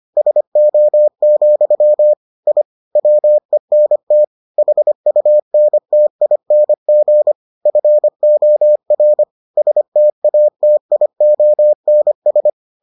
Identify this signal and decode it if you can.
Morse code, heard as a keyed tone, about 25 wpm, 600 Hz. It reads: SO, I WENT HUNTING FOR STATIONS